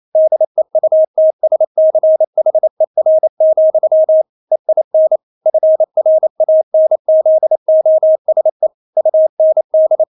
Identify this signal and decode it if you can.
Morse code, heard as a keyed tone, about 28 wpm, 630 Hz. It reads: DEUTSCHER, EIN FRANZOSE UND
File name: Witz9 28wpm